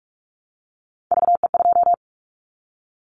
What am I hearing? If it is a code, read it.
Morse code, heard as a keyed tone, about 44 wpm, 700 Hz. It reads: VE2
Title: Callsign Prefixes for Top 25 DXCC - Repeated 3x - 45wpm